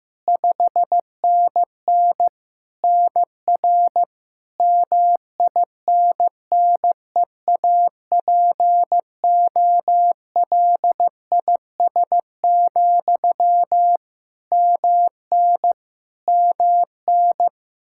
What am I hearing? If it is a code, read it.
Morse code, heard as a keyed tone, about 15 wpm, 700 Hz. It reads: 5NN NR MINNEAPOLIS, MN MN
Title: Practice QSO - Exchange 1 - Beginning - 15wpm